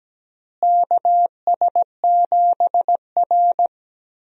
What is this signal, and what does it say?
Morse code, heard as a keyed tone, about 17 wpm, 700 Hz. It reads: KS7R
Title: USA Call Signs - 4 Characters - Repeated 3x - 17wpm